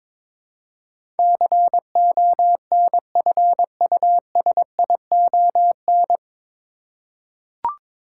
Morse code, heard as a keyed tone, about 22 wpm, 700 Hz. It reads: CONFUSION
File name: 9 Letter Words - First 3 Letters Given - 22wpm